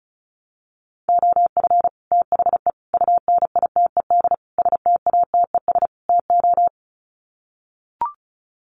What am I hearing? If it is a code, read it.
Morse code, heard as a keyed tone, about 34 wpm, 700 Hz. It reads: OF THE UNITED STATES TO
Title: Sets of 5 Words Encore 35wpm